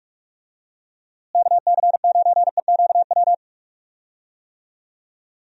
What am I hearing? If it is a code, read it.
Morse code, heard as a keyed tone, about 42 wpm, 700 Hz. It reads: KC9EQW
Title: USA Call Signs - 6 Characters - Repeated 2x - 45wpm